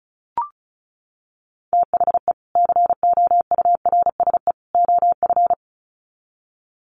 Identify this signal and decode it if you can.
Morse code, heard as a keyed tone, about 34 wpm, 700 Hz. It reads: THE COURSE OF
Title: Sets of 3 Words 35wpm